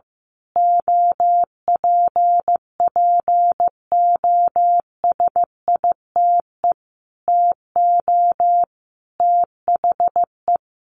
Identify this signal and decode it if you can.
Morse code, heard as a keyed tone, about 15 wpm, 700 Hz. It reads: OPPOSITE TO THE
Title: A Sound Of Thunder 15wpm